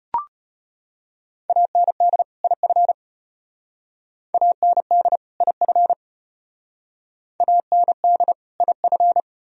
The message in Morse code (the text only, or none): AND IF AND IF AND IF